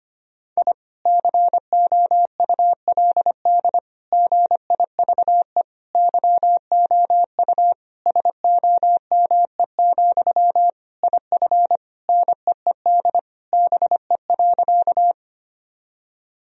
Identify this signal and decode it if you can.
Morse code, heard as a keyed tone, about 25 wpm, 700 Hz. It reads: I COULD GIVE YOU SOME, IF NEED BE.
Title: Sentences from Top 1000 Words - Review Part 1 - 25wpm